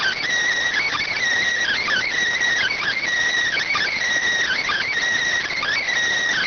Emergency Alert System (SAME) tones (header bursts), heard as harsh buzzing bursts. It is heard from 0.6 to 5.7 s.